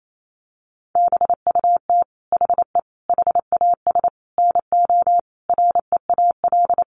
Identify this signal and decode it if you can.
Morse code, heard as a keyed tone, about 28 wpm, 700 Hz. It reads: BUT HE HAS NO REAL
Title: Sentences from Top 1300 Words 28wpm